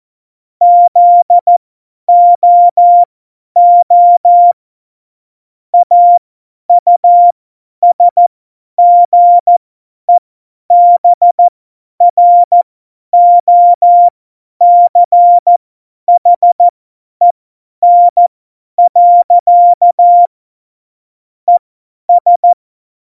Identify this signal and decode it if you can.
Morse code, heard as a keyed tone, about 14 wpm, spaced as for 10 wpm, 700 Hz. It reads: ZOO AUSGEBROCHEN. ES
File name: Morsekurs / Witze / Witz 15. Dezember 2020, 50 F70